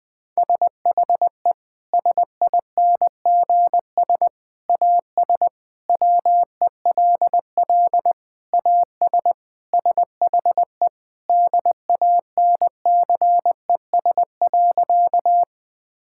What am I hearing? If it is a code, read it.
Morse code, heard as a keyed tone, about 20 wpm, 700 Hz. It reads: SHE SINGS AS WELL AS SHE DANCES.
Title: Sentences with 3-Word Phrases - 20wpm